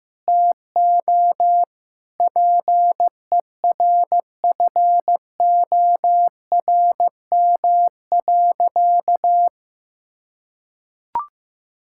Morse code, heard as a keyed tone, about 15 wpm, 700 Hz. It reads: TO PERFORM.
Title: Sentences from Top 1000 Words - Review Part 5 - 15wpm